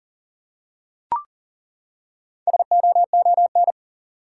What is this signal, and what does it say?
Morse code, heard as a keyed tone, about 40 wpm, 700 Hz. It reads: SOON
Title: Four Letter Words - 40wpm